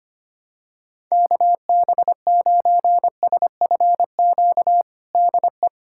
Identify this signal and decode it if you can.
Morse code, heard as a keyed tone, about 25 wpm, 700 Hz. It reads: KB9SFQ DE
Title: USA Callsign DE Callsign - 6 Characters - 25wpm